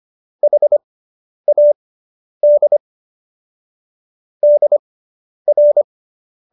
Morse code, heard as a keyed tone, about 25 wpm, spaced as for 10 wpm, 590 Hz. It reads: HAD DR